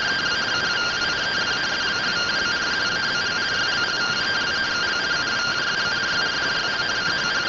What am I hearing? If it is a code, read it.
Two-tone teleprinter signal, about 170 Hz shift.